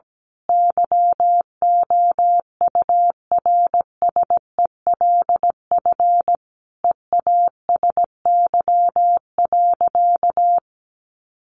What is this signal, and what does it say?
Morse code, heard as a keyed tone, about 17 wpm, 700 Hz. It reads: YOURSELF EASY.
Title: Sentences from Top 700 Words 17wpm